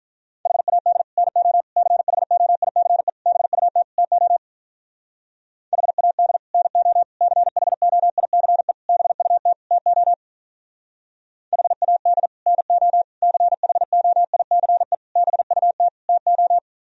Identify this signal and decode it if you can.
Morse code, heard as a keyed tone, about 44 wpm, 700 Hz. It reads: HAD NO CHOICE BUT TO HAD NO CHOICE BUT TO HAD NO CHOICE BUT TO